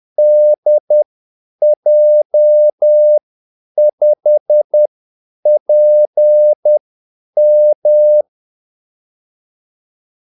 Morse code, heard as a keyed tone, about 10 wpm, spaced as for 8 wpm, 600 Hz. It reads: DJ5PM
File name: Callsigns-10-8wpm